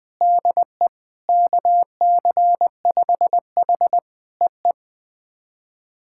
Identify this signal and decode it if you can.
Morse code, heard as a keyed tone, about 20 wpm, 700 Hz. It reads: DE KC5H EE
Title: POTA QSO Practice - 20wpm